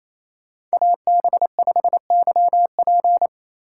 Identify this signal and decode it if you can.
Morse code, heard as a keyed tone, about 36 wpm, 700 Hz. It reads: AB5YP